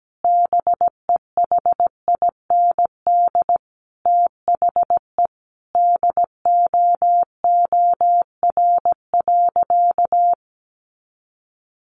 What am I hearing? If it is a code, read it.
Morse code, heard as a keyed tone, about 17 wpm, 700 Hz. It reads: BEHIND THE DOOR.
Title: Sentences from Top 500 Words 17wpm